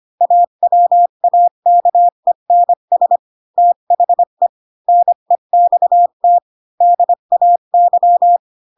Morse code, heard as a keyed tone, about 25 wpm, 700 Hz. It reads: AWAKENS THE NEXT DAY
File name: Pinocchio / Pinocchio ch6 20 25 ik1hge